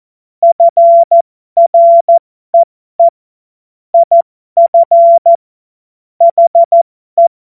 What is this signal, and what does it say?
Morse code, heard as a keyed tone, about 14 wpm, spaced as for 12 wpm, 680 Hz. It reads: FREE IF HE